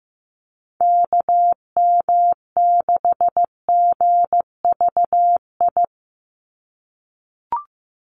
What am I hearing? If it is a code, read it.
Morse code, heard as a keyed tone, about 15 wpm, 700 Hz. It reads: KM6GVI
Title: USA Call Signs 15wpm